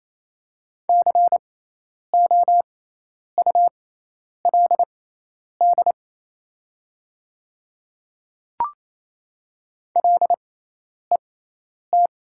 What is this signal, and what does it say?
Morse code, heard as a keyed tone, about 28 wpm, spaced as for 10 wpm, 700 Hz. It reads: COULD LET
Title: 6x Character Spacing - Top 500 Words - 28wpm